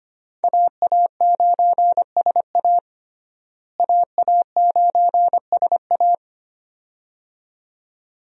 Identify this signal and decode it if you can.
Morse code, heard as a keyed tone, about 25 wpm, 700 Hz. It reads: AA9SA AA9SA
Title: USA Call Signs - 5 Characters - Repeated 2x - 25wpm